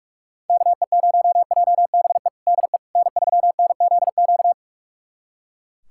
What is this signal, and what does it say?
Morse code, heard as a keyed tone, about 41 wpm, 700 Hz. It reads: KE0JBE DE N3NZQ
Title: USA Callsign DE Callsign 45wpm